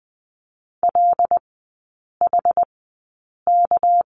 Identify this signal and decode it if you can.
Morse code, heard as a keyed tone, about 20 wpm, spaced as for 8 wpm, 700 Hz. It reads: LHK
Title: 198 Lesson 32e - Letter k - Three Characters 20wpm